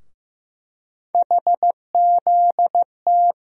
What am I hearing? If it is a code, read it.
Morse code, heard as a keyed tone, about 15 wpm, 700 Hz. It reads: HZT
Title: USA Call Sign Postfixes - 3 Letters 15wpm